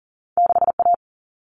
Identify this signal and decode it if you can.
Morse code, heard as a keyed tone, about 39 wpm, 700 Hz. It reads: 6A
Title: All Cut Numbers - 2-Digit 40wpm